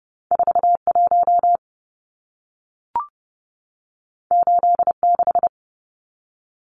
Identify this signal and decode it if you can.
Morse code, heard as a keyed tone, about 30 wpm, 700 Hz. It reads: 41 86
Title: Numbers - 2-Digit 30wpm